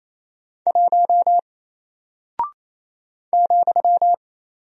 Morse code, heard as a keyed tone, about 27 wpm, 700 Hz. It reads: 1 ,
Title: Single Character (Letter:Number:Punctuation:Prosign) Alt 28wpm